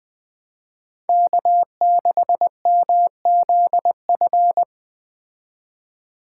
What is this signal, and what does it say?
Morse code, heard as a keyed tone, about 20 wpm, 700 Hz. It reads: K6MZF